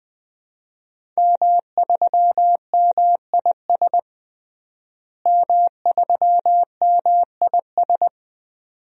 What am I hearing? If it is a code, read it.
Morse code, heard as a keyed tone, about 20 wpm, 700 Hz. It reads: M3MIS M3MIS
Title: DXCC-100 Call Signs - 5 Characters - Repeated 2x - 20wpm